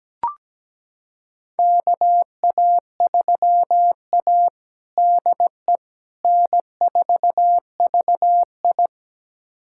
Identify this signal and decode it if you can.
Morse code, heard as a keyed tone, about 17 wpm, 700 Hz. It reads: KA3A DE N4VI
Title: USA Callsign DE Callsign - 4 Characters - 17wpm